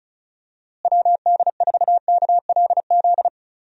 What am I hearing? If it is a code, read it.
Morse code, heard as a keyed tone, about 34 wpm, 700 Hz. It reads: WD4KLZ
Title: USA Call Signs - 6 Characters - Repeated 3x - 35wpm